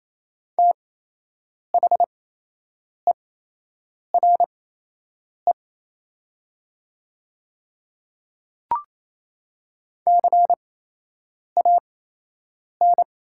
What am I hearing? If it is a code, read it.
Morse code, heard as a keyed tone, about 28 wpm, spaced as for 8 wpm, 700 Hz. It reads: THERE CAN
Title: 8x Character Spacing - Top 200 Words - 28wpm